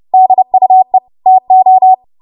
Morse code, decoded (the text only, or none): DUE TO